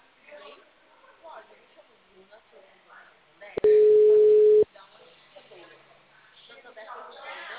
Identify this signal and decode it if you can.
Ringback tone (Europe, 425 Hz), with speech.
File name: out- -2052- - - .12